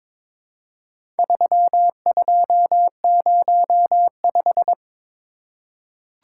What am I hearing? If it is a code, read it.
Morse code, heard as a keyed tone, about 22 wpm, 700 Hz. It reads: 3205
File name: practice / 22wpm / Four Letter-Number Combinations - 22wpm